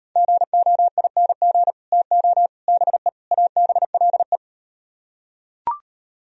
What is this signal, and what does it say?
Morse code, heard as a keyed tone, about 38 wpm, 700 Hz. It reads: GOING TO BE ABLE